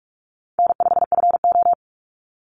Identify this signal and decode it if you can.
Morse code, heard as a keyed tone, about 46 wpm, 700 Hz. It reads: N5FO
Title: Easy Call Signs - 45wpm